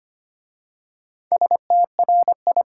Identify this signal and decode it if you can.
Morse code, heard as a keyed tone, about 25 wpm, 700 Hz. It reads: STRI
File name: 5 Letter Words - First 3 Letters Given - 25wpm